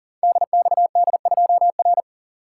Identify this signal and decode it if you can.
Morse code, heard as a keyed tone, about 40 wpm, 700 Hz. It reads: DXD2R
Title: Code Groups - 5 Letter:Numbers - 40wpm